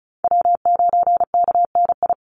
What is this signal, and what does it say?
Morse code, heard as a keyed tone, about 34 wpm, 700 Hz. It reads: W9KNI
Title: Call Signs - 35wpm